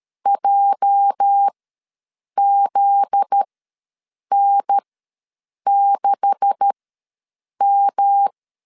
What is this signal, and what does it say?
Morse code, heard as a keyed tone, about 13 wpm, spaced as for 7 wpm, 800 Hz. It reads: JZN6M
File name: LEZIONI / 10 SIGNS / 10A SLASH ws 12wpm 3443